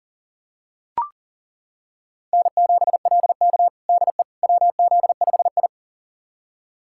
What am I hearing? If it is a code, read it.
Morse code, heard as a keyed tone, about 37 wpm, 700 Hz. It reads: N7LK DE WZ5I
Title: USA Callsign DE Callsign - 4 Characters - 40wpm